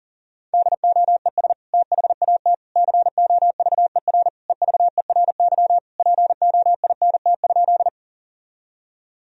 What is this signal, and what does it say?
Morse code, heard as a keyed tone, about 40 wpm, 700 Hz. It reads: DOES THAT COVER EVERY POINT?
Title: Sentences from Top 1000 Words - Review Part 1 - 40wpm